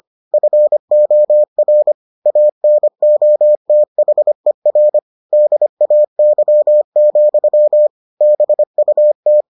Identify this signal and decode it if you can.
Morse code, heard as a keyed tone, about 25 wpm, 600 Hz. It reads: FOR ANOTHER DAY, BUT